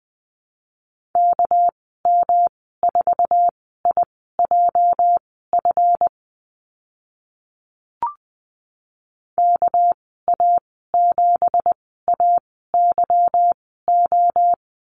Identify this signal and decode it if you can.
Morse code, heard as a keyed tone, about 20 wpm, spaced as for 14 wpm, 700 Hz. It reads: KM4IJF KA7AYO
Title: 219 Lesson 35g - Number 0 - Call Signs 20wpm